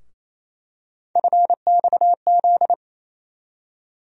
Morse code, heard as a keyed tone, about 28 wpm, 700 Hz. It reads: FXZ
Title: USA Call Sign Postfixes - 3 Letters 28wpm